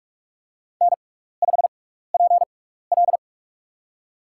Morse code, heard as a keyed tone, about 45 wpm, spaced as for 15 wpm, 700 Hz. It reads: N5PL
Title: USA Call Signs - 4 Characters - 6x Character Spacing - 45wpm